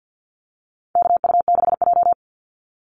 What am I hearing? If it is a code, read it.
Morse code, heard as a keyed tone, about 50 wpm, 700 Hz. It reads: KU6J
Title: NA SOTA Activator Callsigns 50wpm